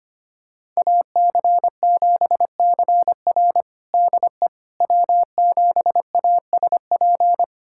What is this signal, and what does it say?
Morse code, heard as a keyed tone, about 25 wpm, 700 Hz. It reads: AC7CR DE W7ASP
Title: USA Callsign DE Callsign - 5 Characters - 25wpm